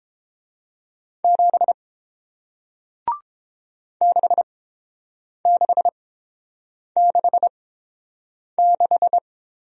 Morse code, heard as a keyed tone, about 30 wpm, spaced as for 7 wpm, 700 Hz. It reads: 7 6666